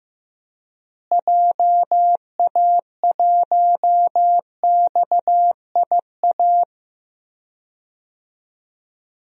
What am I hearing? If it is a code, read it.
Morse code, heard as a keyed tone, about 15 wpm, 700 Hz. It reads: JA1XIA
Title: DXCC-100 Call Signs - 6 Characters - Repeated 2x - 15wpm